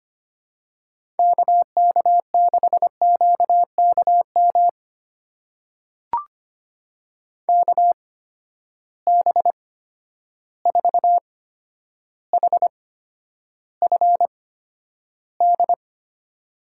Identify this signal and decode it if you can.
Morse code, heard as a keyed tone, about 25 wpm, spaced as for 11 wpm, 700 Hz. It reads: KK6QKM KB4HFD
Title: USA Call Signs - 6 Characters - 8x Character Spacing - 25wpm